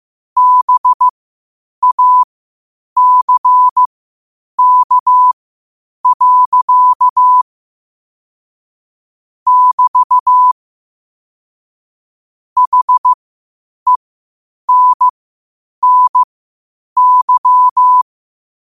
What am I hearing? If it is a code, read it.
Morse code, heard as a keyed tone, about 15 wpm, spaced as for 8 wpm, 1000 Hz. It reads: BACK. <BT> HENNY
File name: qotdfiles / 8wpm